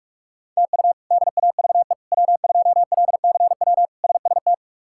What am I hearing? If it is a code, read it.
Morse code, heard as a keyed tone, about 46 wpm, 700 Hz. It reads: TU DAVE W2LCW SST
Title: K1USN SST QSO Practice - 45wpm